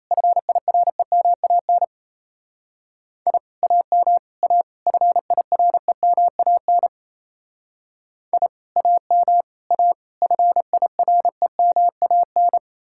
Morse code, heard as a keyed tone, about 36 wpm, 700 Hz. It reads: FIREMAN I AM A FIREMAN I AM A FIREMAN